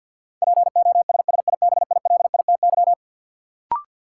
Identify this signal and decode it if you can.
Morse code, heard as a keyed tone, about 52 wpm, 700 Hz. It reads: POSSIBILITY
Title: Sets of 3 Words - New Word Drill - 50wpm